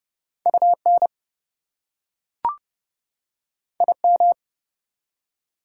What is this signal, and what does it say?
Morse code, heard as a keyed tone, about 30 wpm, 700 Hz. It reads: UN IM
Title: Prefixes and Suffixes 30wpm